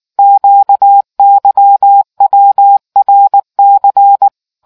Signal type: Morse code